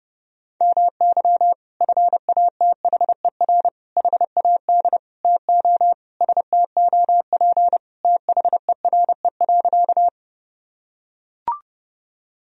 Morse code, heard as a keyed tone, about 30 wpm, 700 Hz. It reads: MY FATHER HAD TO STOP THERE.
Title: Sentences from Top 1000 Words - Review Part 1 - 30wpm